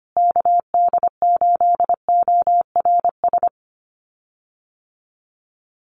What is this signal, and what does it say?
Morse code, heard as a keyed tone, about 25 wpm, 700 Hz. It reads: KD8ORS
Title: USA Call Signs Repeated 2x 25wpm